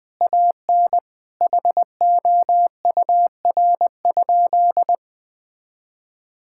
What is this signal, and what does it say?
Morse code, heard as a keyed tone, about 21 wpm, 700 Hz. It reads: AN HOUR?
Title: Sentences from Top 1000 Words - Review Part 5 - 20wpm